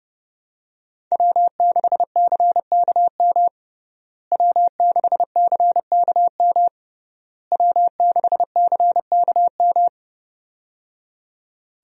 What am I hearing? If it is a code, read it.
Morse code, heard as a keyed tone, about 30 wpm, 700 Hz. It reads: W6CKM W6CKM W6CKM
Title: USA Call Signs - 5 Characters - Repeated 3x - 30wpm